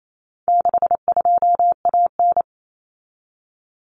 Morse code, heard as a keyed tone, about 28 wpm, 700 Hz. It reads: 62AN
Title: Common Cut Numbers - 4-Digit 28wpm